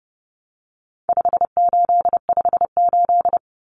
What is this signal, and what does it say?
Morse code, heard as a keyed tone, about 30 wpm, 700 Hz. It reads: 5858
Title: Numbers - 4-Digit 30wpm